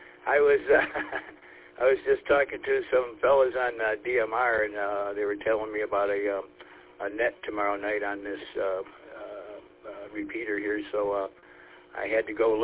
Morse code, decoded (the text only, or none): none